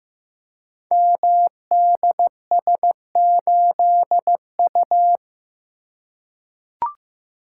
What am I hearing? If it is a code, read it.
Morse code, heard as a keyed tone, about 15 wpm, 700 Hz. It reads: MDS8U
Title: Code Groups - 5 Letter:Numbers - 15wpm